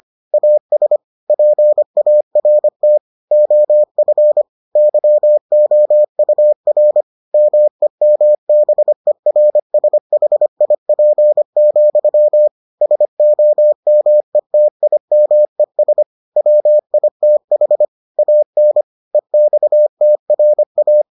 Morse code, heard as a keyed tone, about 25 wpm, 600 Hz. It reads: AS PART OF YOUR MEMBERSHIP, SOMETIMES WITH AN EXTRA